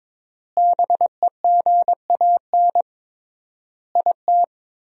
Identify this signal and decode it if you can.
Morse code, heard as a keyed tone, about 22 wpm, 700 Hz. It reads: BEGAN IT
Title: 3x Word-Spacing - Sentences from Top 300 Words - 22wpm